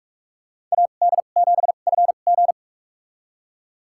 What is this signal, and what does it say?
Morse code, heard as a keyed tone, about 43 wpm, spaced as for 31 wpm, 700 Hz. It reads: AD7FG
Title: USA Call Signs - 2x Character Spacing - 45wpm